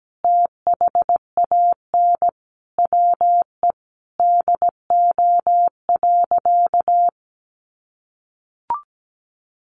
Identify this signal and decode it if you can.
Morse code, heard as a keyed tone, about 17 wpm, 700 Hz. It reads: THAN WE DO.
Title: Sentences from Top 100 Words 17wpm